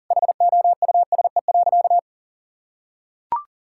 Morse code, heard as a keyed tone, about 40 wpm, 700 Hz. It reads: HOUSE.
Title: Sentences from Top 1000 Words - Review Part 6 - 40wpm